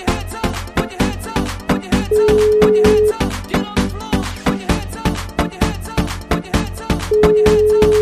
Ringback tone (Europe, 425 Hz).